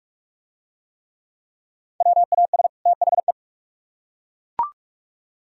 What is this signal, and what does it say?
Morse code, heard as a keyed tone, about 39 wpm, 700 Hz. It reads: WAS THE